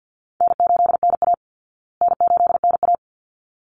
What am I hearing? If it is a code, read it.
Morse code, heard as a keyed tone, about 50 wpm, 700 Hz. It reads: N8NA N8NA
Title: Easy Call Signs Repeated 2x - 50wpm